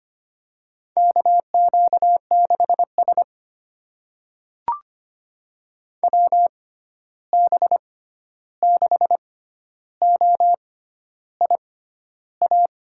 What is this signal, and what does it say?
Morse code, heard as a keyed tone, about 25 wpm, spaced as for 11 wpm, 700 Hz. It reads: KQ6S WB6OIA
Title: USA Call Signs - 6x Character Spacing - 25wpm